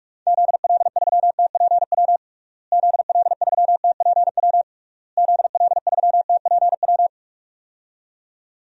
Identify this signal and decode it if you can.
Morse code, heard as a keyed tone, about 46 wpm, 700 Hz. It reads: ZL3TPW ZL3TPW ZL3TPW
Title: DXCC-100 Call Signs - 6 Characters - Repeated 3x - 45wpm